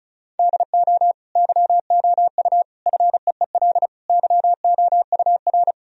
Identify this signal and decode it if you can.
Morse code, heard as a keyed tone, about 36 wpm, 700 Hz. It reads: DO YOU FEEL YOUR
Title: Sentences from Top 1000 Words - Review Part 1 - 35wpm